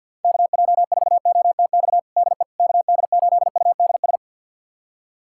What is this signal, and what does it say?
Morse code, heard as a keyed tone, about 52 wpm, 700 Hz. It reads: KJ4OTX DE KD8UDS
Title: USA Callsign DE Callsign - 6 Characters - 50wpm